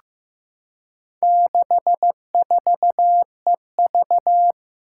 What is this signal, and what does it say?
Morse code, heard as a keyed tone, about 15 wpm, 700 Hz. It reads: 64EV
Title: Four Challenging Letter-Number Combinations 15wpm